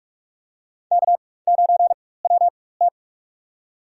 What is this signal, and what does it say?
Morse code, heard as a keyed tone, about 44 wpm, spaced as for 21 wpm, 700 Hz. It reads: K9WT
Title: USA Call Signs - 4 Characters - 4x Character Spacing - 45wpm